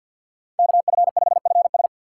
Morse code, heard as a keyed tone, about 49 wpm, 700 Hz. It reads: KF5FS
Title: USA Call Signs - 5 Characters - 2x Character Spacing - 50wpm